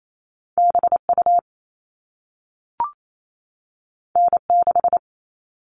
Morse code, heard as a keyed tone, about 28 wpm, 700 Hz. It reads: BU N6
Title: All Cut Numbers - 2-Digit 28wpm